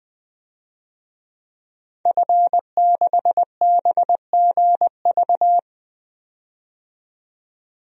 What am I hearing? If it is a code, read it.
Morse code, heard as a keyed tone, about 20 wpm, 700 Hz. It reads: F6BGV
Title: DXCC-100 Call Signs - 5 Characters - 20wpm